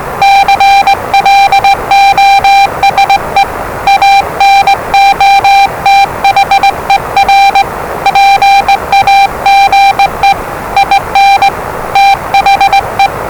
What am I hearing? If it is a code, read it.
Morse code, heard as a keyed tone, about 20 wpm, 790 Hz, with noise. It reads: CLOSE ANOTHER PAGE IN THE
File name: WellingtonRadioZlwCloseDownOnHfAndMf / 01-Zlw 4277khz 417.5 qrt until 0925z